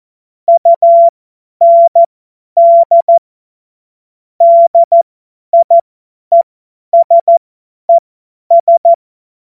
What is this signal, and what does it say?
Morse code, heard as a keyed tone, about 14 wpm, spaced as for 10 wpm, 680 Hz. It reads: UND DIESES
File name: Morsekurs / Witze / Witz 18. Juli 2023 50 F70